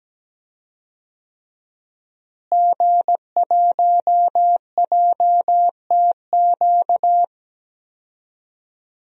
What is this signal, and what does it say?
Morse code, heard as a keyed tone, about 17 wpm, 700 Hz. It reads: G1JTQ
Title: DXCC-100 Call Signs - 5 Characters - Repeated 3x - 17wpm